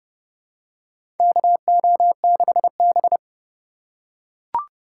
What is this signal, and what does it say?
Morse code, heard as a keyed tone, about 30 wpm, 700 Hz. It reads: KO6B
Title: USA Call Signs - 2x Character Spacing - 30wpm